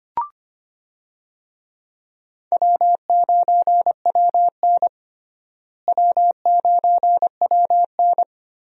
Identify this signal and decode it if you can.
Morse code, heard as a keyed tone, about 25 wpm, 700 Hz. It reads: W9WN W9WN
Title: USA Call Signs - 4 Characters - Repeated 3x - 25wpm